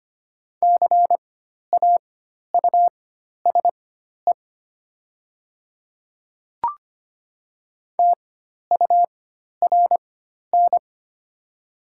Morse code, heard as a keyed tone, about 25 wpm, spaced as for 12 wpm, 700 Hz. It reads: CAUSE TURN
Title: 4x Character Spacing - Top 200 Words - 25wpm